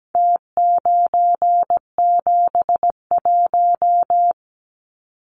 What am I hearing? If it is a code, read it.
Morse code, heard as a keyed tone, about 17 wpm, 700 Hz. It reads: T971
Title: Numbers - Contest Style - 4-Digit 17wpm